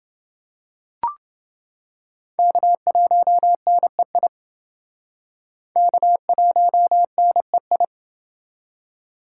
Morse code, heard as a keyed tone, about 29 wpm, 700 Hz. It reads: K1NEI K1NEI